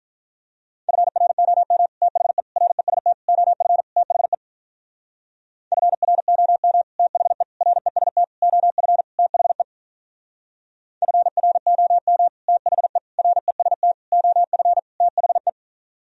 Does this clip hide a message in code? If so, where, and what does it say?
Yes: Morse code from 0.9 to 15.5 s, heard as a keyed tone, about 45 wpm, 700 Hz. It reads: FROM THE REST OF THE FROM THE REST OF THE FROM THE REST OF THE